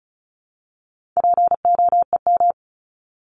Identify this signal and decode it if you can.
Morse code, heard as a keyed tone, about 34 wpm, 700 Hz. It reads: POEM
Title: Top 1751-2000 Words 35wpm